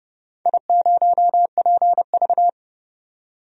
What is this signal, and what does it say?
Morse code, heard as a keyed tone, about 30 wpm, 700 Hz. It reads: I0PV
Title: DXCC-100 Call Signs - 4 Characters - 30wpm